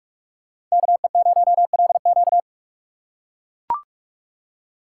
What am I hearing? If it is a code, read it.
Morse code, heard as a keyed tone, about 42 wpm, 700 Hz. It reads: KE0LQ
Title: USA Call Signs - 5 Characters - 8x Character Spacing - 45wpm